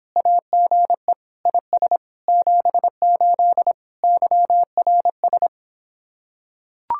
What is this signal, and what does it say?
Morse code, heard as a keyed tone, about 26 wpm, 700 Hz. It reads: AGE IS 78 YRS